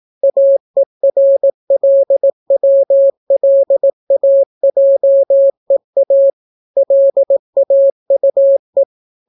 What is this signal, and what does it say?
Morse code, heard as a keyed tone, about 18 wpm, 550 Hz. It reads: AERLWLAJEA LAUE